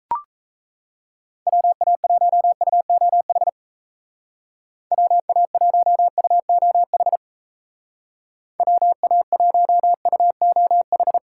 Morse code, heard as a keyed tone, about 34 wpm, 700 Hz. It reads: WA1UOH WA1UOH WA1UOH
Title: Speed Racing - USA 2x3 Call Signs 28wpm